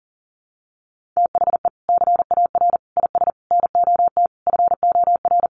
Morse code, heard as a keyed tone, about 40 wpm, 700 Hz. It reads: THE CAR IS NOT FOR
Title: Sentences from Top 1400 Words 40wpm